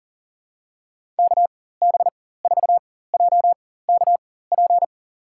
Morse code, heard as a keyed tone, about 40 wpm, spaced as for 19 wpm, 700 Hz. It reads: KB4JKP
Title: USA Call Signs - 6 Characters - 4x Character Spacing - 40wpm